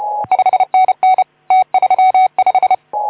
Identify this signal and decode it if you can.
Morse code, heard as a keyed tone, about 32 wpm, 750 Hz. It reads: 5NN T35
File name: Archiv / CUC181 / OK2ZC CUC 181